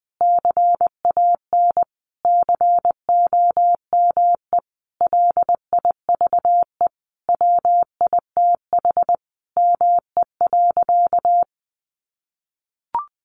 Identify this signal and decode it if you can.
Morse code, heard as a keyed tone, about 20 wpm, 700 Hz. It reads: CAN COME LIVE WITH ME.
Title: Sentences from Top 1300 Words 20wpm